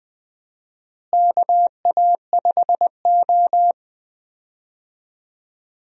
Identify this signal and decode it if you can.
Morse code, heard as a keyed tone, about 20 wpm, 700 Hz. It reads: KA5O
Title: USA Call Signs - 4 Characters - Repeated 2x - 20wpm